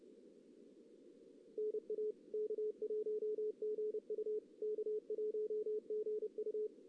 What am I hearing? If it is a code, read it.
Morse code, heard as a keyed tone, about 30 wpm, 440 Hz. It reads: NA K1GU K1GU
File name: 2016 JAN CW NAQP / 2016 Jan naqp k5ot - 0-4